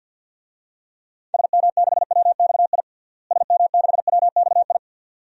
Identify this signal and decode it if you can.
Morse code, heard as a keyed tone, about 52 wpm, 700 Hz. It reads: SM6WXI SM6WXI
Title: DXCC-100 Call Signs - 6 Characters - Repeated 2x - 50wpm